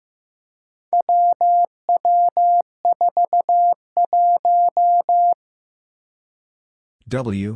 Morse code, heard as a keyed tone, about 15 wpm, 700 Hz. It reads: WW41
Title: Four Challenging Letter-Number Combinations 15wpm